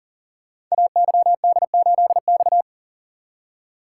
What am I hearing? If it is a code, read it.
Morse code, heard as a keyed tone, about 38 wpm, 690 Hz. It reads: AYD8X
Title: Code Groups - 5 Letter:Numbers - 40wpm